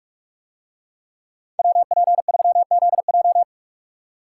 Morse code, heard as a keyed tone, about 45 wpm, 700 Hz. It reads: WP3ZJ